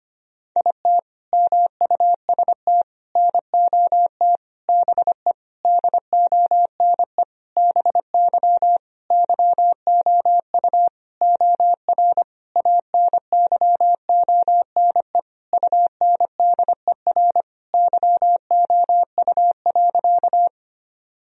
Morse code, heard as a keyed tone, about 25 wpm, 700 Hz. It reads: IT MUST NOT BE DONE BY YOU OR ANYONE UNDER YOU.
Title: Sentences from Top 1000 Words - Review Part 1 - 25wpm